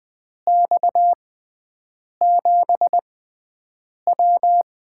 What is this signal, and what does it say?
Morse code, heard as a keyed tone, about 20 wpm, spaced as for 7 wpm, 700 Hz. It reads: X7W
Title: USA Call Signs - 4 Characters - 6x Character Spacing - 20wpm